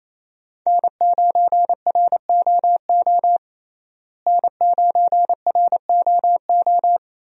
Morse code, heard as a keyed tone, about 27 wpm, 700 Hz. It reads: N9ROO N9ROO
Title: USA Call Signs - 5 Characters - Repeated 3x - 28wpm